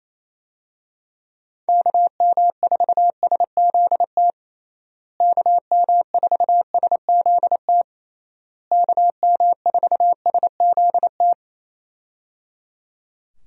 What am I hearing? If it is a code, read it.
Morse code, heard as a keyed tone, about 28 wpm, 700 Hz. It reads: KM4SZT KM4SZT KM4SZT
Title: USA Call Signs - 6 Characters - Repeated 3x - 28wpm